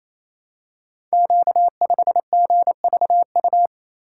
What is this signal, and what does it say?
Morse code, heard as a keyed tone, about 29 wpm, 700 Hz. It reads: Q5GVU
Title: Code Groups - 5 Letter:Numbers - 28wpm